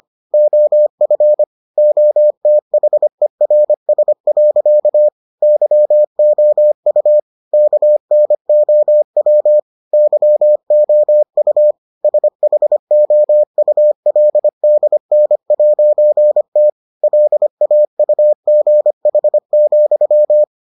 Morse code, heard as a keyed tone, about 25 wpm, 600 Hz. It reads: OF OTHERS. YOU KNOW YOU SHOULDN'T LAUGH,